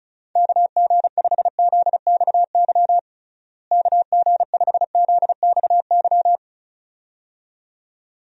Morse code, heard as a keyed tone, about 36 wpm, 700 Hz. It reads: KG5ZXY KG5ZXY
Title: USA Call Signs - 6 Characters - Repeated 2x - 35wpm